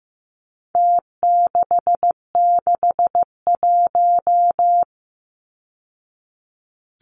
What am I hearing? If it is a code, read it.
Morse code, heard as a keyed tone, about 15 wpm, 700 Hz. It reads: T661
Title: Numbers - Contest Style - 4-Digit 15wpm